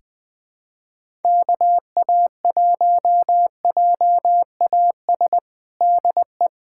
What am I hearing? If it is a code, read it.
Morse code, heard as a keyed tone, about 20 wpm, 700 Hz. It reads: KA1JAS DE
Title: USA Callsign DE Callsign - 6 Characters - 20wpm